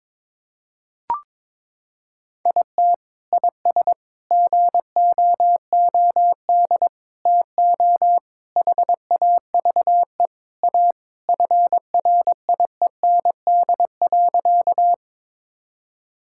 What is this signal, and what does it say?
Morse code, heard as a keyed tone, about 22 wpm, 700 Hz. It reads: IT IS GOOD TO HAVE A FRIEND.
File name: practice / 22wpm / Sentences with 3-Word Phrases - 22wpm